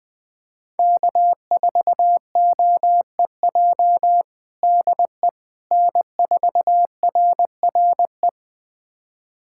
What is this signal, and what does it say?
Morse code, heard as a keyed tone, about 20 wpm, 700 Hz. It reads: K4OEJ DE N4RRE
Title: USA Callsign DE Callsign - 5 Characters - 20wpm